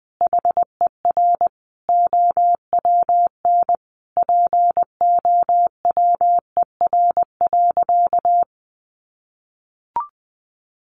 Morse code, heard as a keyed tone, about 20 wpm, 700 Hz. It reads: HER OWN POWER.
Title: Sentences from Top 1200 Words 20wpm